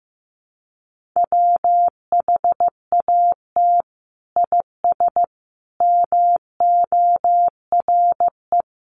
Morse code, heard as a keyed tone, about 15 wpm, 700 Hz. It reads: WHAT IS MORE
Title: Sentences from Top 1400 Words 15wpm